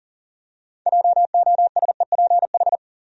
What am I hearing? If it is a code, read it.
Morse code, heard as a keyed tone, about 40 wpm, 700 Hz. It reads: JOSEPH
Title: USA Common Names - 40wpm